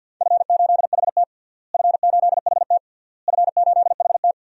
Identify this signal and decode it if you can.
Morse code, heard as a keyed tone, about 52 wpm, 700 Hz. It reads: F8HT F8HT F8HT
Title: DXCC-100 Call Signs - 4 Characters - Repeated 3x - 50wpm